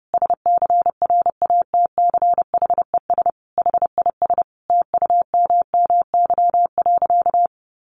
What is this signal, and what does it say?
Morse code, heard as a keyed tone, about 30 wpm, 700 Hz. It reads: SCRATCHES HIS TUMMY.
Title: Easy Phrases 30wpm